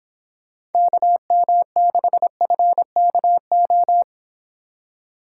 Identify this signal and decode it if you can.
Morse code, heard as a keyed tone, about 28 wpm, 700 Hz. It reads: KM6FKO